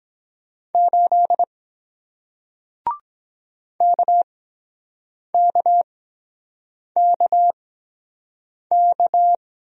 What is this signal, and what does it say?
Morse code, heard as a keyed tone, about 24 wpm, spaced as for 7 wpm, 700 Hz. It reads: 8 KKKK